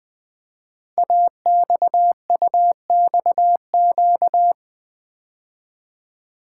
Morse code, heard as a keyed tone, about 20 wpm, 700 Hz. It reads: AXUXQ
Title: Code Groups - 5 Letters - 20wpm